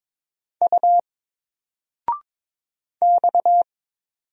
Morse code, heard as a keyed tone, about 22 wpm, 700 Hz. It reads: U X
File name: Single-Letter-Number-22wpm